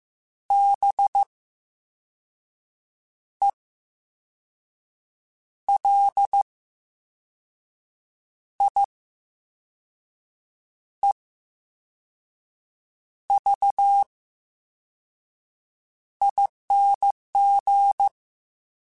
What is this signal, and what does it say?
Morse code, heard as a keyed tone, about 15 wpm, spaced as for 5 wpm, 800 Hz. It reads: BELIEVING
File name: Special INGs-mp3s / Special INGs-mp3s / VerySlowToFaster Ing @13